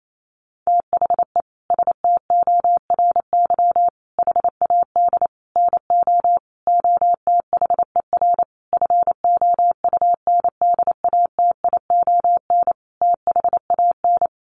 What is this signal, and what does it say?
Morse code, heard as a keyed tone, about 28 wpm, 700 Hz. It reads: THE STORY HAD NO OTHER FOUNDATION THAN
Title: Sentences from Top 1200 Words 28wpm